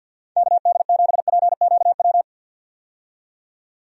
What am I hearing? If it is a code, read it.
Morse code, heard as a keyed tone, about 51 wpm, 700 Hz. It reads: KD7PQW
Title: USA Call Signs - 6 Characters - 50wpm